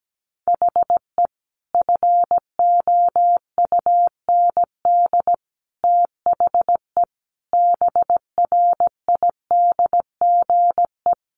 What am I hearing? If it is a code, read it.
Morse code, heard as a keyed tone, about 17 wpm, 700 Hz. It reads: HE FOUND THE BRIDGE
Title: Sentences from Top 1200 Words 17wpm